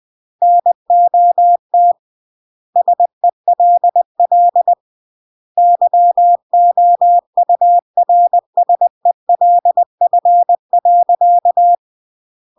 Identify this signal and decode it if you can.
Morse code, heard as a keyed tone, about 20 wpm, 700 Hz. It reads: NOT SELL YOURSELF.
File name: CW / jodorowsky / 82 commandments 20-20